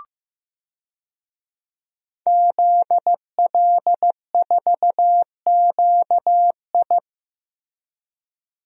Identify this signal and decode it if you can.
Morse code, heard as a keyed tone, about 15 wpm, 700 Hz. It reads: ZL4QI
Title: DXCC-100 Call Signs - 5 Characters - Repeated 3x - 15wpm